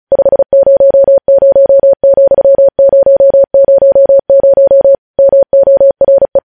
Morse code, heard as a keyed tone, about 36 wpm, 550 Hz. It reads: 500,000 MORE
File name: podcast / News headlines-1x-35x35